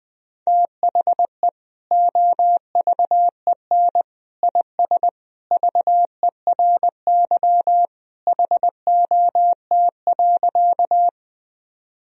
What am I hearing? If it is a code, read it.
Morse code, heard as a keyed tone, about 20 wpm, 700 Hz. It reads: THE OVEN IS VERY HOT.
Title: Sentences from Top 1000 Words - Review Part 4 - 20wpm